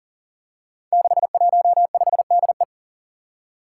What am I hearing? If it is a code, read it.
Morse code, heard as a keyed tone, about 40 wpm, 700 Hz. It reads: 615DE
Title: Five Challenging Letter-Number Combinations 40wpm